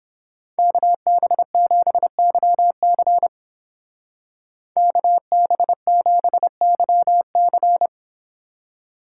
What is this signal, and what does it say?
Morse code, heard as a keyed tone, about 29 wpm, 700 Hz. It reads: KB7YC KB7YC